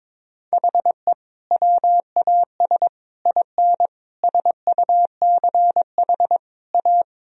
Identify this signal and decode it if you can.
Morse code, heard as a keyed tone, about 22 wpm, 700 Hz. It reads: HE WAS IN SUCH A
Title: Sentences from Top 1000 Words - Review Part 2 - 22wpm